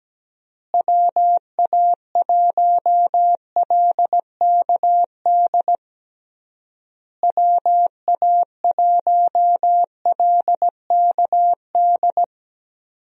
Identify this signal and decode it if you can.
Morse code, heard as a keyed tone, about 17 wpm, 700 Hz. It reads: WA1LKD WA1LKD
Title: USA Call Signs - 6 Characters - Repeated 3x - 17wpm